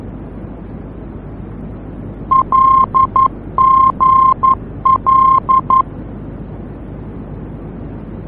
Morse code, heard as a keyed tone, about 11 wpm, 1060 Hz. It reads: LGL
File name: france / mx / LGL-F-VOR